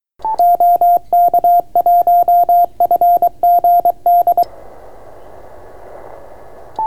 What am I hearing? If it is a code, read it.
Morse code, heard as a keyed tone, about 23 wpm, 700 Hz. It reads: OK1FGD